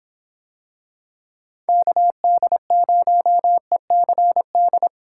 Morse code, heard as a keyed tone, about 26 wpm, 700 Hz. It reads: KD0ECD